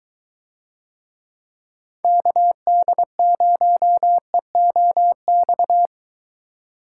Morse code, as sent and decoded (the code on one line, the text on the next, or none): -.- -.. ----- . --- -..-
KD0EOX